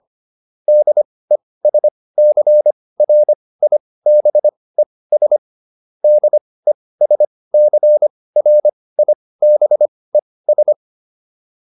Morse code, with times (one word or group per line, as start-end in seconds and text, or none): 0.7-5.4 DESCRIBES
6.0-10.7 DESCRIBES